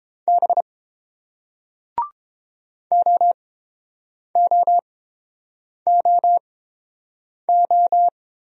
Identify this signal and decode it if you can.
Morse code, heard as a keyed tone, about 30 wpm, spaced as for 7 wpm, 700 Hz. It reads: B OOOO